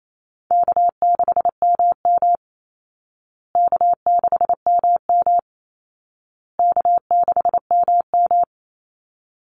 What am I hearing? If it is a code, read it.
Morse code, heard as a keyed tone, about 28 wpm, 700 Hz. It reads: K6MM K6MM K6MM
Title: Call Signs Repeated 2x - 28wpm